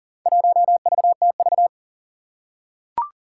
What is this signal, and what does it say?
Morse code, heard as a keyed tone, about 39 wpm, 700 Hz. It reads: 1VTV
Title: Four Challenging Letter-Number Combinations 40wpm